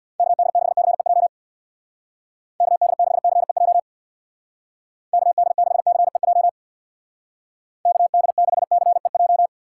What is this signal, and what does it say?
Morse code, heard as a keyed tone, about 61 wpm, 700 Hz. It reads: KD6CEJ KD6CEJ KD6CEJ KD6CEJ
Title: Speed Racing - USA 2x3 Call Signs 50wpm